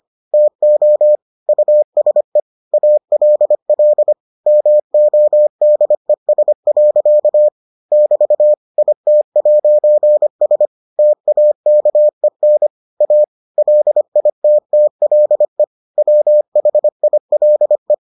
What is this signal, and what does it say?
Morse code, heard as a keyed tone, about 25 wpm, 600 Hz. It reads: TO USE ALL MODES. <BT> IT'S TAKEN A LITTLE WHILE